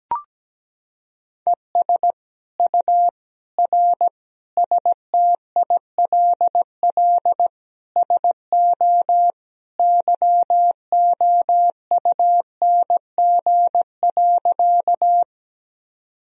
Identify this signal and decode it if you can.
Morse code, heard as a keyed tone, about 17 wpm, 700 Hz. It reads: ES U R STILL SO YOUNG.
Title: Sentences from Top 400 Words and Abbreviations 17wpm